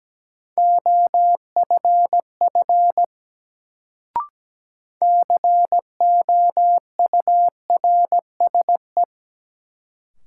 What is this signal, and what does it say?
Morse code, heard as a keyed tone, about 17 wpm, 700 Hz. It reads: OFF COURSE
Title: Sets of 2 Words - New Word Drill - 17wpm